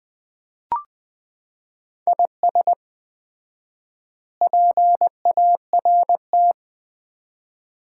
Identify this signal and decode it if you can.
Morse code, heard as a keyed tone, about 21 wpm, 700 Hz. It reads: IS PART
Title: 4x Word-Spacing - Sets of 4 Words - 20wpm